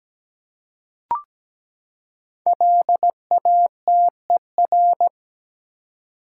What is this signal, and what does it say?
Morse code, heard as a keyed tone, about 17 wpm, 700 Hz. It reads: LATER
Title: Five Letter Words - 17wpm